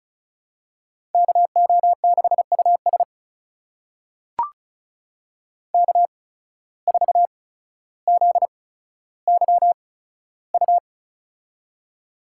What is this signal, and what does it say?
Morse code, heard as a keyed tone, about 36 wpm, spaced as for 15 wpm, 700 Hz. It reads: KO6US K4ZYU
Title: USA Call Signs - 5 Characters - 8x Character Spacing - 35wpm